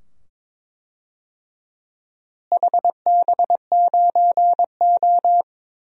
Morse code, heard as a keyed tone, about 22 wpm, 700 Hz. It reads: HB9O
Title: DXCC-100 Call Signs - 4 Characters - Repeated 2x - 22wpm